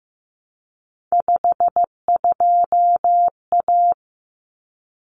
Morse code, heard as a keyed tone, about 15 wpm, 700 Hz. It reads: 52A
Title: Common Cut Numbers - 3-Digit 15wpm